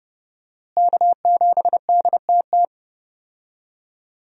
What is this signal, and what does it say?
Morse code, heard as a keyed tone, about 30 wpm, 700 Hz. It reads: K7DTT
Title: USA Call Signs - 5 Characters - Repeated 3x - 30wpm